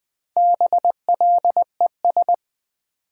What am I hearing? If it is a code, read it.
Morse code, heard as a keyed tone, about 20 wpm, 700 Hz. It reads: BLES
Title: Build A Word - 1 Letter At A Time - 3 to 5-Letter Words - 20wpm